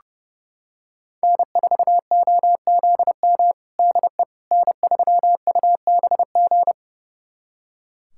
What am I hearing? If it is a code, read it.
Morse code, heard as a keyed tone, about 30 wpm, 700 Hz. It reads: N4OZM DE N3UBG
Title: USA Callsign DE Callsign - 5 Characters - 30wpm